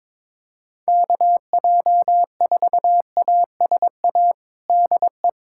Morse code, heard as a keyed tone, about 22 wpm, 700 Hz. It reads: KJ4ASA DE
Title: USA Callsign DE Callsign - 6 Characters - 22wpm